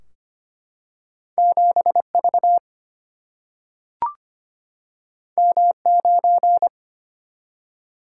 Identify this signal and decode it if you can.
Morse code, heard as a keyed tone, about 25 wpm, 700 Hz. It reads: 7V M9
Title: Single-Letter Single-Number 25wpm